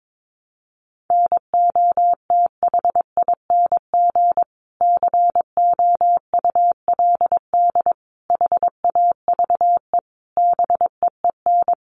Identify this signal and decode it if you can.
Morse code, heard as a keyed tone, about 22 wpm, 700 Hz. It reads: NOTHING COULD HAVE BEEN
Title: Sentences from Top 500 Words 22wpm